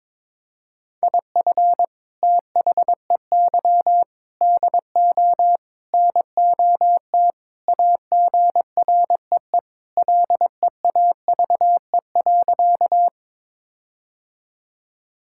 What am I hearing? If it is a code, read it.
Morse code, heard as a keyed tone, about 22 wpm, 700 Hz. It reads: IF THEY DO NOT AGREE LEAVE.
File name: practice / 22wpm / Sentences with 4-Word Phrases - 22wpm